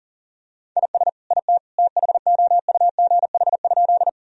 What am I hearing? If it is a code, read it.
Morse code, heard as a keyed tone, about 37 wpm, 700 Hz. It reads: IS IT THOUGH?
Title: Sentences from Top 1000 Words - Review Part 2 - 40wpm